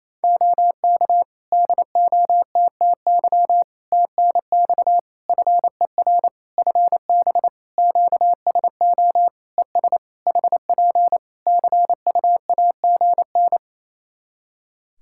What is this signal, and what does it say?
Morse code, heard as a keyed tone, about 28 wpm, 700 Hz. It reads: OK DOTTY TNX FER FB QSO ES HP CUAGN
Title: Practice QSO - Exchange 4 - Ending - 28wpm